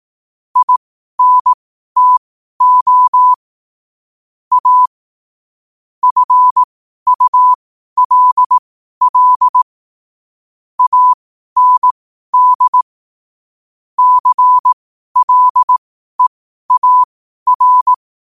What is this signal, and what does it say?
Morse code, heard as a keyed tone, about 18 wpm, spaced as for 12 wpm, 1000 Hz. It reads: INTO A FULL AND CLEAR
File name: qotdfiles / 12wpm